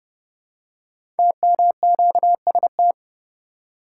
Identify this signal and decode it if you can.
Morse code, heard as a keyed tone, about 30 wpm, 700 Hz. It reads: TMQST
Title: Code Groups - 5 Letter:Numbers - 30wpm